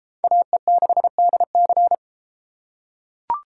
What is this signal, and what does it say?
Morse code, heard as a keyed tone, about 29 wpm, 700 Hz. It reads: AE6DC